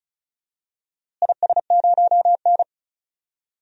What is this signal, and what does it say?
Morse code, heard as a keyed tone, about 34 wpm, 700 Hz. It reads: IS0N
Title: DXCC-100 Call Signs - 4 Characters - Repeated 2x - 35wpm